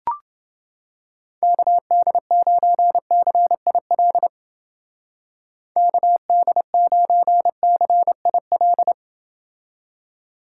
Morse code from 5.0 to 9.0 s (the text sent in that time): KD9CIL